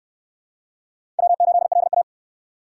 Morse code, heard as a keyed tone, about 66 wpm, 700 Hz. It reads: W8LA